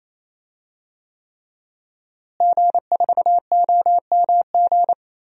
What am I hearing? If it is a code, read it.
Morse code, heard as a keyed tone, about 28 wpm, 700 Hz. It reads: G4OMG
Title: DXCC-100 Call Signs - 5 Characters - Repeated 3x - 28wpm